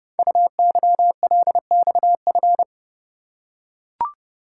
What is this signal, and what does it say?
Morse code, heard as a keyed tone, about 30 wpm, 700 Hz. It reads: UYLXF
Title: Code Groups - 5 Letters - 30wpm